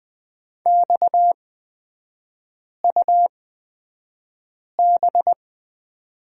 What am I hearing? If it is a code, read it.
Morse code, heard as a keyed tone, about 20 wpm, spaced as for 5 wpm, 690 Hz. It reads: XUB
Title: 229 Lesson 37e - Letter x - Three Characters 20wpm